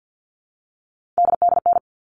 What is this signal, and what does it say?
Morse code, heard as a keyed tone, about 50 wpm, 700 Hz. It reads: DDN
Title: All Cut Numbers - 3-Digit 50wpm